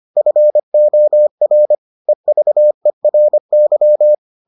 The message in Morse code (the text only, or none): FOR EVERY